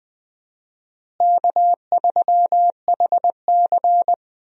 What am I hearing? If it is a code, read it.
Morse code, heard as a keyed tone, about 20 wpm, 700 Hz. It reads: K3HC
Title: USA Call Signs - 4 Characters - Repeated 3x - 20wpm